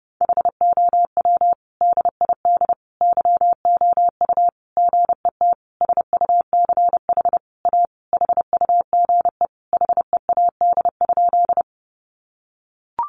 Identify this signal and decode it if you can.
Morse code, heard as a keyed tone, about 30 wpm, 700 Hz. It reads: HOW DID YOU GET SUCH A HUGE HEAD?
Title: Sentences from Top 1100 Words 30wpm